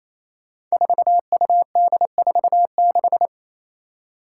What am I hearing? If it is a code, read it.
Morse code, heard as a keyed tone, about 28 wpm, 700 Hz. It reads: 4UD46
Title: Five Challenging Letter-Number Combinations 28wpm